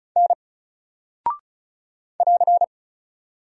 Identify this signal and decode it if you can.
Morse code, heard as a keyed tone, about 35 wpm, 700 Hz. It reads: N <AR>
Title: Single Character (Letter:Number:Punctuation:Prosign) 35wpm